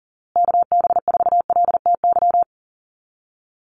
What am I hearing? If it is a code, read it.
Morse code, heard as a keyed tone, about 40 wpm, 700 Hz. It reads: KB4LTY
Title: USA Call Signs Repeated 2x 40wpm